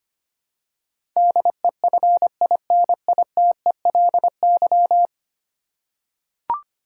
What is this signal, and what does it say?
Morse code, heard as a keyed tone, about 25 wpm, 700 Hz. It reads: DEFINITELY
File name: Ten Letter Words - 25wpm